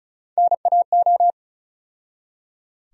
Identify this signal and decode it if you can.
Morse code, heard as a keyed tone, about 36 wpm, 700 Hz. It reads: NAO
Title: USA Call Sign Postfixes - 3 Letters 35wpm